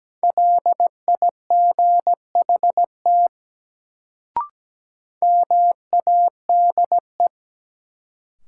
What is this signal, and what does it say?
Morse code, heard as a keyed tone, about 17 wpm, 700 Hz. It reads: LIGHT MADE
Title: Sentences from Top 200 Words - New Word Drill - 17wpm